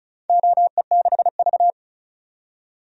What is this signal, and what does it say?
Morse code, heard as a keyed tone, about 36 wpm, 700 Hz. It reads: OE6V
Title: DXCC-100 Call Signs - 4 Characters - 35wpm